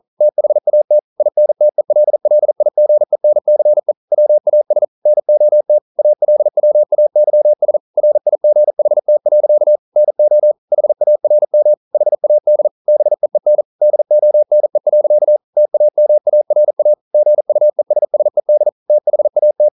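Morse code, heard as a keyed tone, about 42 wpm, 600 Hz. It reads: THAT INTELLIGENCE WAS NOT ALWAYS RIGHT. NO HARM HAD BEEN DONE. TAMARA GUESSED THAT
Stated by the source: Morse code practice at 41 wpm, from CW med Gnister.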